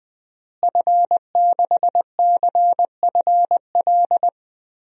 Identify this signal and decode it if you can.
Morse code, heard as a keyed tone, about 21 wpm, 700 Hz. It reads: F6CFL
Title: DXCC-100 Call Signs - 5 Characters - Repeated 2x - 20wpm